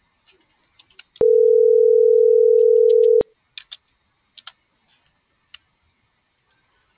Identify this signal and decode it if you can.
Ringback tone (North America), 440 + 480 Hz.